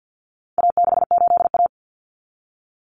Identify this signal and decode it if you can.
Morse code, heard as a keyed tone, about 50 wpm, 700 Hz. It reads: A68A
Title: Common Cut Numbers - 4-Digit 50wpm